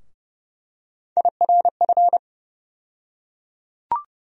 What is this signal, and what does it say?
Morse code, heard as a keyed tone, about 30 wpm, 700 Hz. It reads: IRF
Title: USA Call Sign Postfixes - 3 Letters 30wpm